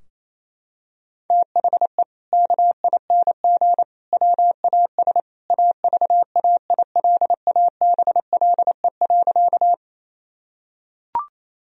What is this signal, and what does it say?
Morse code, heard as a keyed tone, about 28 wpm, 700 Hz. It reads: THE KING WAS AVAILABLE.
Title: Sentences from Top 1000 Words - Review Part 3 - 28wpm